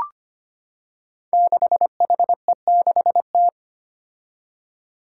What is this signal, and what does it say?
Morse code, heard as a keyed tone, about 25 wpm, 700 Hz. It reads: 6HE6T
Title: Five Challenging Letter-Number Combinations 25wpm